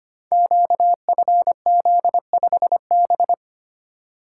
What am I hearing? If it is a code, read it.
Morse code, heard as a keyed tone, about 25 wpm, 700 Hz. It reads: QFZ5B
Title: Code Groups - 5 Letter:Numbers - 25wpm